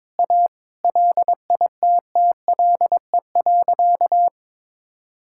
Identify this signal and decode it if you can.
Morse code, heard as a keyed tone, about 21 wpm, 700 Hz. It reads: A LITTLE.
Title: Sentences from Top 1000 Words - Review Part 6 - 22wpm